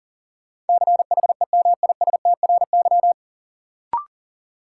Morse code, heard as a keyed tone, about 40 wpm, 700 Hz. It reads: CHEMISTRY
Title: Sentences from Top 1000 Words - Review Part 6 - New Word Drill - 40wpm